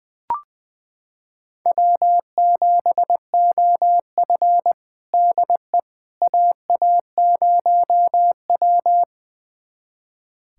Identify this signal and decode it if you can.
Morse code, heard as a keyed tone, about 20 wpm, 700 Hz. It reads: W7OF DE AA0W
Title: USA Callsign DE Callsign - 4 Characters - 20wpm